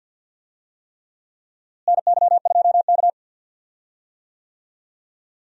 Morse code, heard as a keyed tone, about 48 wpm, 700 Hz. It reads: NY2K
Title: USA Call Signs - 4 Characters - Repeated 3x - 50wpm